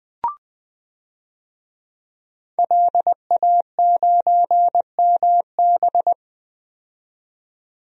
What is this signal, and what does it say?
Morse code, heard as a keyed tone, about 20 wpm, 700 Hz. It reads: LA9MB
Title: DXCC-100 Call Signs - 5 Characters - 20wpm